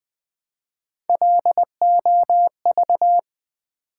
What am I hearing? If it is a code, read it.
Morse code, heard as a keyed tone, about 20 wpm, 700 Hz. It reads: LOV
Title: Build A Word - 1 Letter At A Time - 4-Letter Words - 20wpm